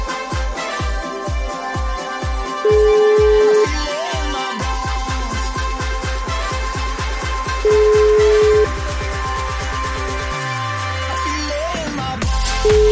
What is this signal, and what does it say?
Ringback tone (Europe, 425 Hz), with noise.